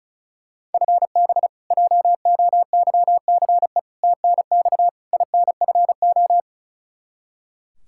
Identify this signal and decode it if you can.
Morse code, heard as a keyed tone, about 36 wpm, 700 Hz. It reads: FB JOYCE TNX INFO
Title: K1USN SST QSO Practice 35wpm